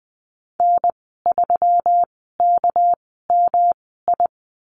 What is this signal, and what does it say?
Morse code, heard as a keyed tone, about 20 wpm, spaced as for 14 wpm, 700 Hz. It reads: N3KMI
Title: 200 Lesson 32g - Letter k - Call Signs 20wpm